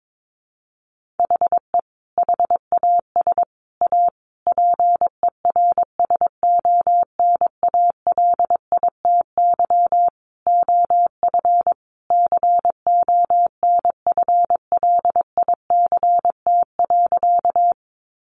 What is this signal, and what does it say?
Morse code, heard as a keyed tone, about 22 wpm, 700 Hz. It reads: HE HAS A PERSONALITY OF CONFLICT.
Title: Sentences from Top 1400 Words 22wpm